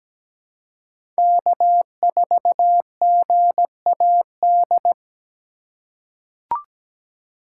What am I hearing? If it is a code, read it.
Morse code, heard as a keyed tone, about 17 wpm, 700 Hz. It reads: K4GAD
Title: USA Call Signs - 5 Characters - 4x Character Spacing - 17wpm